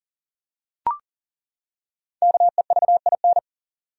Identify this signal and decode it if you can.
Morse code, heard as a keyed tone, about 40 wpm, 700 Hz. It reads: KEVIN
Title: USA Common Names - Max 5 Letters - 40wpm